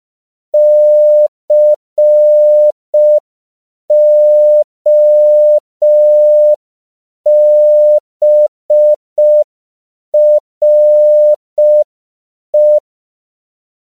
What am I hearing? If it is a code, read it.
Morse code, heard as a keyed tone, about 5 wpm, 600 Hz. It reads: COBRE
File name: CW / 5PPM